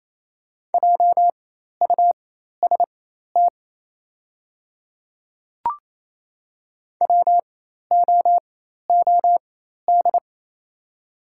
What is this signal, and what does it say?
Morse code, heard as a keyed tone, about 28 wpm, spaced as for 13 wpm, 700 Hz. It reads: JUST WOOD
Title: 4x Character Spacing - Top 300 Words - 28wpm